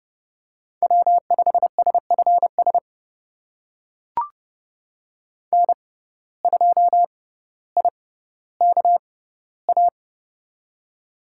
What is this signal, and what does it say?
Morse code, heard as a keyed tone, about 30 wpm, spaced as for 15 wpm, 700 Hz. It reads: W5SFS N2IKA
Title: USA Call Signs - 5 Characters - 6x Character Spacing - 30wpm